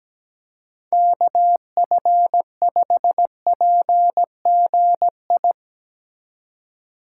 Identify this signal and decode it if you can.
Morse code, heard as a keyed tone, about 17 wpm, 700 Hz. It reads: KF5PGI
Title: USA Call Signs - 6 Characters - 2x Character Spacing - 17wpm